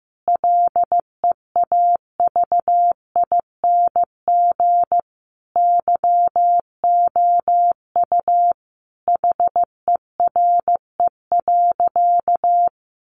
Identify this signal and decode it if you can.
Morse code, heard as a keyed tone, about 15 wpm, 700 Hz. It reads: LEAVING YOU HERE.
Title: A Sound Of Thunder 15wpm